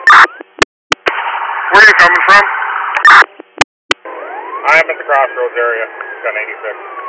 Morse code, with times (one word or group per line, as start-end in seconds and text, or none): none